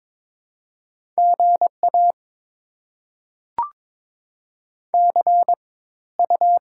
Morse code, heard as a keyed tone, about 22 wpm, spaced as for 14 wpm, 690 Hz. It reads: GA CU
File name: Top 20 QSO Elements - 4x Character Spacing - 22wpm